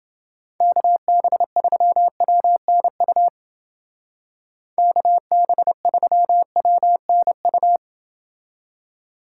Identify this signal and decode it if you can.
Morse code, heard as a keyed tone, about 28 wpm, 700 Hz. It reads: KB3WNU KB3WNU